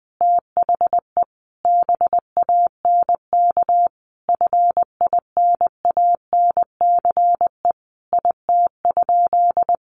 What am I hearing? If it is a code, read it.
Morse code, heard as a keyed tone, about 20 wpm, 700 Hz. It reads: THE BANK FINANCE IT?
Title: Sentences from Top 1500 Words 20wpm